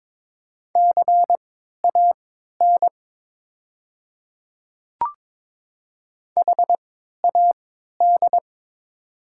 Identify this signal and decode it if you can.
Morse code, heard as a keyed tone, about 22 wpm, spaced as for 12 wpm, 700 Hz. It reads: CAN HAD
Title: 3x Character Spacing - Top 100 Words - 22wpm